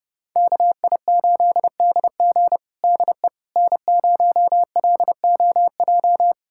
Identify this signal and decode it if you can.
Morse code, heard as a keyed tone, about 32 wpm, 700 Hz. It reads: KI8DG DE N0LOJ
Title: USA Callsign DE Callsign - 5 Characters - 30wpm